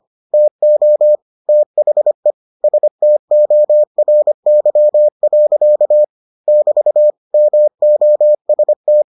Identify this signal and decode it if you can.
Morse code, heard as a keyed tone, about 25 wpm, 600 Hz. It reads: TO THE STORY. <BT> MOST